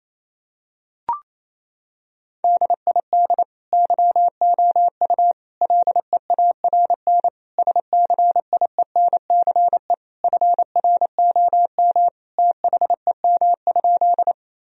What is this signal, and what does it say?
Morse code, heard as a keyed tone, about 28 wpm, 700 Hz. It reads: DID YOU LEARN SCIENCE FROM THEM?
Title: Sentences from Top 1000 Words - Review Part 2 - 28wpm